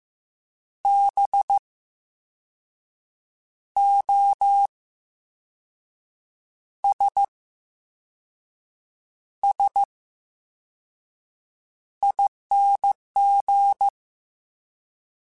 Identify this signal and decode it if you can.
Morse code, heard as a keyed tone, about 15 wpm, spaced as for 5 wpm, 800 Hz. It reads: BOSSING
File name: Special INGs / Special INGs-mp3s / Slow ing @13